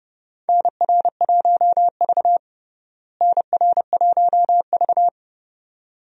Morse code, heard as a keyed tone, about 30 wpm, 700 Hz. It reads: NR1V NR1V
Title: USA Call Signs - 4 Characters - Repeated 2x - 30wpm